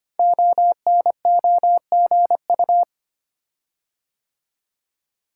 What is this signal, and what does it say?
Morse code, heard as a keyed tone, about 25 wpm, 700 Hz. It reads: ONOGU
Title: Code Groups - 5 Letters - 25wpm